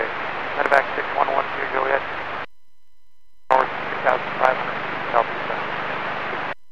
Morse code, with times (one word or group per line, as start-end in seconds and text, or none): none